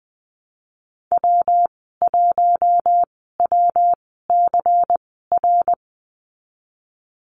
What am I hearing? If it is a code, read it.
Morse code, heard as a keyed tone, about 19 wpm, spaced as for 14 wpm, 700 Hz. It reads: W1WCR
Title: 128 Lesson 21g - Number 6 - Call Signs 20wpm